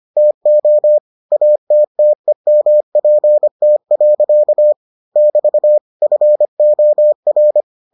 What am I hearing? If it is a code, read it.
Morse code, heard as a keyed tone, about 25 wpm, 600 Hz. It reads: TO ATTEMPT. <BT> FOR